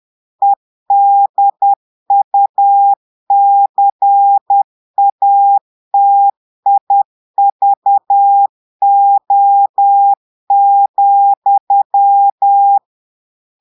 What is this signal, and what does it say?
Morse code, heard as a keyed tone, about 10 wpm, 800 Hz. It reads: EDUCATIVO,
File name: Portugais - texte en clair 1 - vitesse - 10 mots minute